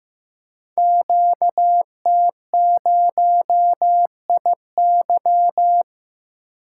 Morse code, heard as a keyed tone, about 15 wpm, 700 Hz. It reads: QT0IY
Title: Code Groups - 5 Letter:Numbers - 15wpm